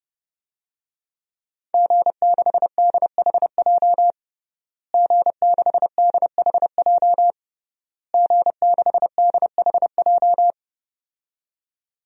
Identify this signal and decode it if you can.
Morse code, heard as a keyed tone, about 32 wpm, 700 Hz. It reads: G6DHJ G6DHJ G6DHJ
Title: DXCC-100 Call Signs - 5 Characters - Repeated 3x - 30wpm